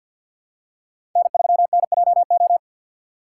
Morse code, heard as a keyed tone, about 48 wpm, 700 Hz. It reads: N3NJO
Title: USA Call Signs - 5 Characters - 4x Character Spacing - 50wpm